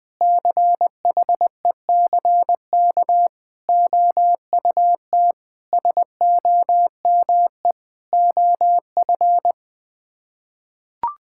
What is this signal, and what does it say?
Morse code, heard as a keyed tone, about 20 wpm, 700 Hz. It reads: CHECK OUT SOME OF
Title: Sets of 9 Words - 20wpm